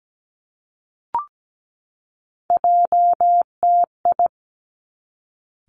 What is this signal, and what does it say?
Morse code, heard as a keyed tone, about 17 wpm, 700 Hz. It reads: JTI
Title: Three Challenging Letter-Number Combinations 17wpm